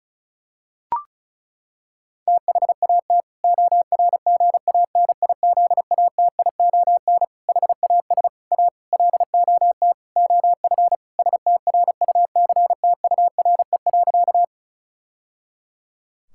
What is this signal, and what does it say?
Morse code, heard as a keyed tone, about 36 wpm, 700 Hz. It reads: THAT ORGANIZATION HAS A LOT OF STRUCTURE.
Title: Sentences from Top 1000 Words - Review Part 6 - 35wpm